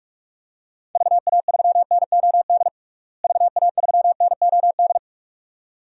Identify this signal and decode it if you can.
Morse code, heard as a keyed tone, about 48 wpm, 700 Hz. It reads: VA3NOD VA3NOD
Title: DXCC-100 Call Signs - 6 Characters - Repeated 2x - 45wpm